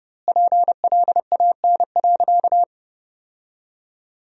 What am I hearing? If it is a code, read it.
Morse code, heard as a keyed tone, about 30 wpm, 700 Hz. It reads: PLAN.
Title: 2x Word-Spacing - Sentences from Top 500 Words - 30wpm